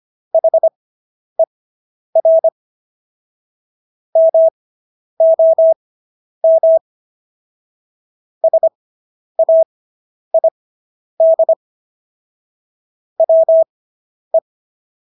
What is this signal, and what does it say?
Morse code, heard as a keyed tone, about 25 wpm, spaced as for 10 wpm, 650 Hz. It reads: HER MOM SAID WE